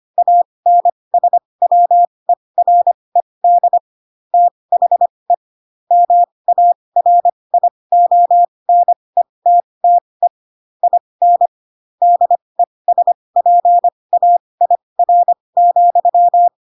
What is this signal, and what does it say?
Morse code, heard as a keyed tone, about 25 wpm, spaced as for 20 wpm, 700 Hz. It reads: ANSWERED THE MARIONETTE IN DESPAIR,
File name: Pinocchio / Pinocchio ch7 15 25 ik1hge